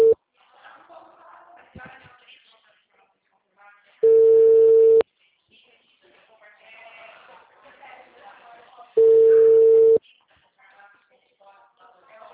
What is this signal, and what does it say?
Ringback tone (China, 450 Hz), with speech.
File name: out- -2013- - - .99475